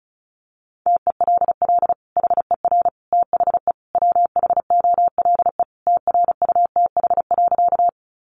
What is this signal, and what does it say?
Morse code, heard as a keyed tone, about 34 wpm, 700 Hz. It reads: TELL HER THE WHOLE TRUTH.
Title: Sentences from Top 800 Words 35wpm